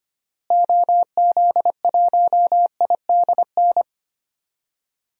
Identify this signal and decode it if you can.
Morse code, heard as a keyed tone, about 25 wpm, 700 Hz. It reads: OZ1IDN
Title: DXCC-100 Call Signs - 6 Characters - Repeated 3x - 25wpm